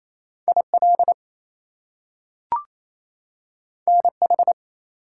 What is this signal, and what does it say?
Morse code, heard as a keyed tone, about 28 wpm, 700 Hz. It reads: IL NH
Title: US States and Canadian Provinces 28wpm